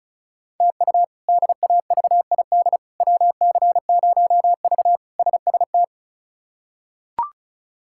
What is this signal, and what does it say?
Morse code, heard as a keyed tone, about 36 wpm, 700 Hz. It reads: TU DAVID WC0V SST
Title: K1USN SST QSO Practice - 35wpm